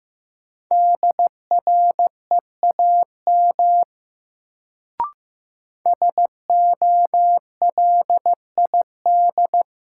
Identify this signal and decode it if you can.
Morse code, heard as a keyed tone, about 15 wpm, 700 Hz. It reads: DREAM SOLID
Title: Sentences from Top 1500 Words - New Word Drill Alt - 15wpm